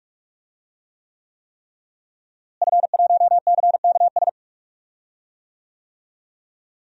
Morse code, heard as a keyed tone, about 46 wpm, 700 Hz. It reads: F1CKS
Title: DXCC-100 Call Signs - 5 Characters - Repeated 3x - 45wpm